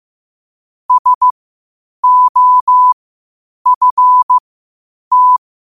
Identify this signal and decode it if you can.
Morse code, heard as a keyed tone, about 15 wpm, spaced as for 8 wpm, 1000 Hz. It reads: SOFT
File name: qotdfiles / 8wpm